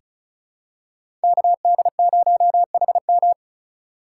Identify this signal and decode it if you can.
Morse code, heard as a keyed tone, about 36 wpm, 700 Hz. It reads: KD0HM
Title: USA Call Signs - 3x Character Spacing - 35wpm